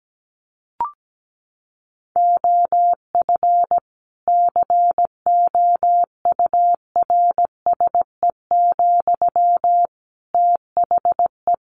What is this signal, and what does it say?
Morse code, heard as a keyed tone, about 17 wpm, 700 Hz. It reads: OF COURSE, THE
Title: Sentences from Top 1500 Words 17wpm